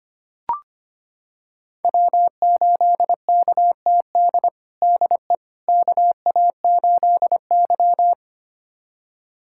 Morse code, heard as a keyed tone, about 25 wpm, 700 Hz. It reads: W8KTD DE KA8Y
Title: USA Callsign DE Callsign 25wpm